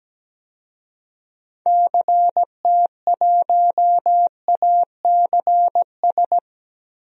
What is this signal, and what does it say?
Morse code, heard as a keyed tone, about 17 wpm, 700 Hz. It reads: CT1ACS
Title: DXCC-100 Call Signs - 6 Characters - Repeated 2x - 17wpm